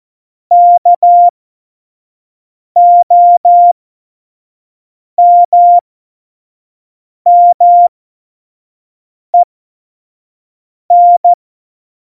Morse code, heard as a keyed tone, about 14 wpm, spaced as for 5 wpm, 700 Hz. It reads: KOMMEN
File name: 25 F70